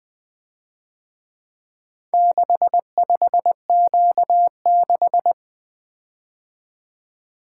Morse code, heard as a keyed tone, about 23 wpm, 700 Hz. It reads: 65Q6
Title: Speed-Racing - Four Letter-Number Combinations - 20wpm